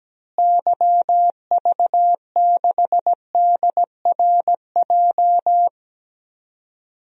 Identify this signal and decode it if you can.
Morse code, heard as a keyed tone, about 17 wpm, 700 Hz. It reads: YV6DRJ
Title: DXCC-100 Call Signs - 6 Characters - Repeated 2x - 17wpm